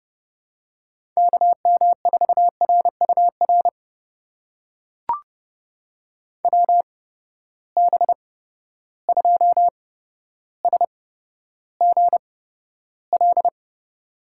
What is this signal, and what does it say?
Morse code, heard as a keyed tone, about 30 wpm, spaced as for 13 wpm, 700 Hz. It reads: KM4RUR WB2SGL
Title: USA Call Signs - 6 Characters - 8x Character Spacing - 30wpm